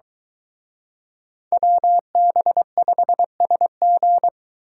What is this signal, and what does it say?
Morse code, heard as a keyed tone, about 23 wpm, 700 Hz. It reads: WB5SG